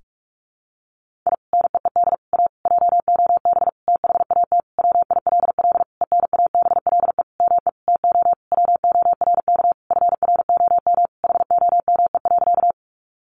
Morse code, heard as a keyed tone, about 44 wpm, 700 Hz. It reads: I NEED A JOB THAT WILL ENABLE ME TO WORK FROM HOME.
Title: Sentences from Top 1500 Words 45wpm